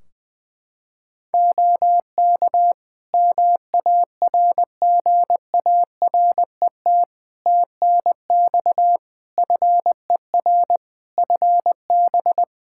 Morse code, heard as a keyed tone, about 20 wpm, 700 Hz. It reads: OK MARGARET TNX FER FB
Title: Practice QSO - Exchange 4 - Ending - 20wpm